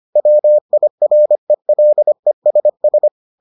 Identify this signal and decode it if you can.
Morse code, heard as a keyed tone, about 25 wpm, 600 Hz. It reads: WIRELESS